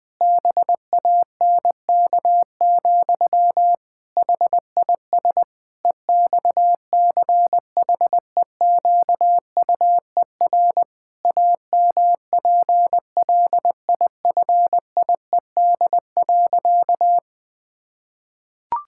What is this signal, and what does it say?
Morse code, heard as a keyed tone, about 20 wpm, 700 Hz. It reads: BANK, HIS EXCHEQUER AMPLIFIED.
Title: Pangram Challenge v2 - 20wpm